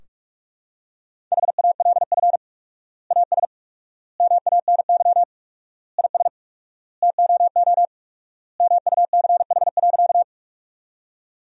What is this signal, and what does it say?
Morse code, heard as a keyed tone, about 46 wpm, 700 Hz. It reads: HALF AS MANY IS TOO MUCH.
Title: 4x Word-Spacing - Sentences from Top 400 Words - 45wpm